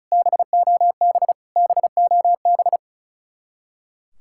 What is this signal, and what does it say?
Morse code, heard as a keyed tone, about 46 wpm, 700 Hz. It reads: BOB BOB